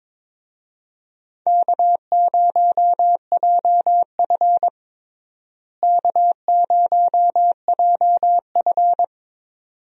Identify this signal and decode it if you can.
Morse code, heard as a keyed tone, about 22 wpm, 700 Hz. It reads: K0JF K0JF
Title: USA Call Signs - 4 Characters - Repeated 2x - 22wpm